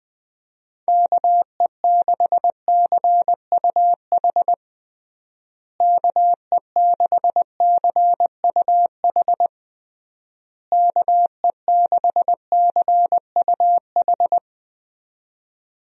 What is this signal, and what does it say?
Morse code, heard as a keyed tone, about 20 wpm, 700 Hz. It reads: KE6CUH KE6CUH KE6CUH
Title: USA Call Signs - 6 Characters - Repeated 3x - 20wpm